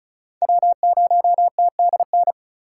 Morse code, heard as a keyed tone, about 36 wpm, 700 Hz. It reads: W0TDN
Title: USA Call Signs - 5 Characters - 2x Character Spacing - 35wpm